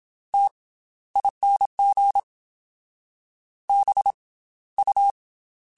Morse code, heard as a keyed tone, about 27 wpm, 800 Hz. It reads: T ING B U
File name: Special INGs / Special INGs-mp3s / Slow ing @25